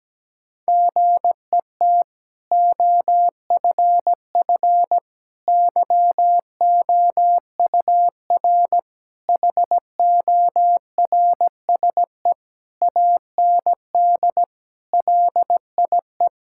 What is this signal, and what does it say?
Morse code, heard as a keyed tone, about 17 wpm, 700 Hz. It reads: GET OFF YOUR HORSE AND LIE
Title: Sentences from Top 1600 Words 17wpm